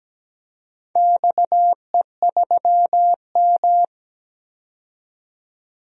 Morse code, heard as a keyed tone, about 17 wpm, 700 Hz. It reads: XE3M
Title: DXCC-100 Call Signs - 4 Characters - 17wpm